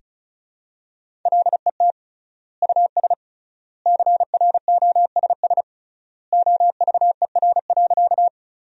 Morse code, heard as a keyed tone, about 34 wpm, 700 Hz. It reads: LET US CROSS OVER.
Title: 3x Word-Spacing - Sentences from Top 300 Words - 35wpm